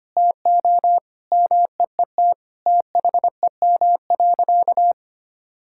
Morse code, heard as a keyed tone, about 25 wpm, 700 Hz. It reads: TO MEET THEM.
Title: Sentences from Top 1000 Words - Review Part 6 - 25wpm